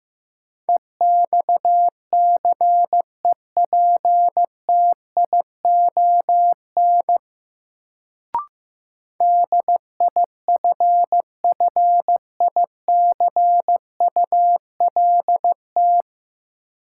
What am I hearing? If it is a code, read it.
Morse code, heard as a keyed tone, about 15 wpm, 700 Hz. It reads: EXCEPTION DIFFICULT
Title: Sets of 4 Words - New Word Drill - 15wpm